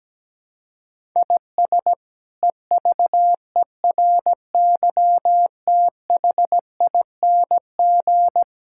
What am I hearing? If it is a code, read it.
Morse code, heard as a keyed tone, about 17 wpm, 700 Hz. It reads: IS EVERYTHING
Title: Sentences from Top 1000 Words - Review Part 6 - 17wpm